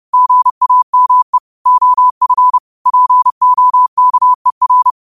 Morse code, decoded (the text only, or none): GAME OF POKER